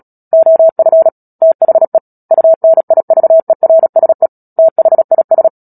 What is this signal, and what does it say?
Morse code, heard as a keyed tone, about 37 wpm, 650 Hz. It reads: OF THE UNIVERSE THIS
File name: Book Two / 10-Chapter 10 - Epilogue - 36 WPM